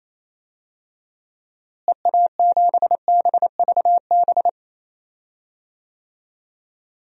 Morse code, heard as a keyed tone, about 27 wpm, 700 Hz. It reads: EA7BVB
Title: DXCC-100 Call Signs - 6 Characters - Repeated 2x - 28wpm